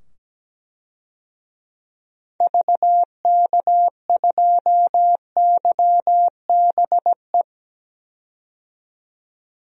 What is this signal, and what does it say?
Morse code, heard as a keyed tone, about 17 wpm, 700 Hz. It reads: VK2YBE
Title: DXCC-100 Call Signs - 6 Characters - 17wpm